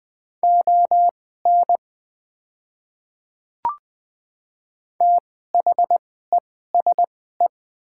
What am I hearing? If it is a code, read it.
Morse code, heard as a keyed tone, about 20 wpm, spaced as for 15 wpm, 700 Hz. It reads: ON THESE
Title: 2x Character Spacing - Top 100 Words - 20wpm